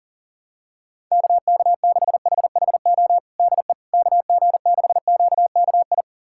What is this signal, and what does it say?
Morse code, heard as a keyed tone, about 40 wpm, 700 Hz. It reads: KK6HHO DE KG6QKI
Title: USA Callsign DE Callsign 40wpm